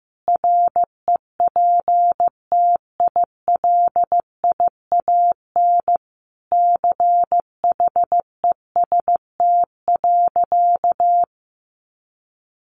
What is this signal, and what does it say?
Morse code, heard as a keyed tone, about 15 wpm, 700 Hz. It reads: REPTILIAN CHEST.
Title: A Sound Of Thunder 15wpm